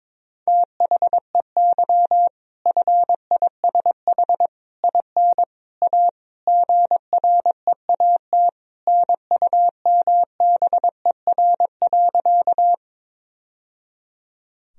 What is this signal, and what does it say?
Morse code, heard as a keyed tone, about 22 wpm, 700 Hz. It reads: THEY FISH IN A GREAT NUMBER.
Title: Sentences from Top 1000 Words - Review Part 1 - 22wpm